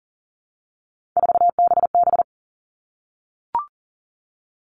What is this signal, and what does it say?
Morse code, heard as a keyed tone, about 40 wpm, 700 Hz. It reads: 4BB
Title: All Cut Numbers - 3-Digit 40wpm